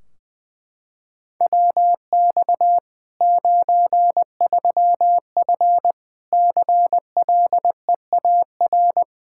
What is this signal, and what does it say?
Morse code, heard as a keyed tone, about 20 wpm, 700 Hz. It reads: WX 93F CLEAR
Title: Practice QSO - Exchange 2 - Info - 20wpm